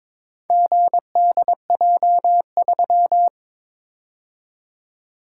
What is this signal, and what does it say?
Morse code, heard as a keyed tone, about 22 wpm, 700 Hz. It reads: GDJ3
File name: practice / 22wpm / Four Letter-Number Combinations - 22wpm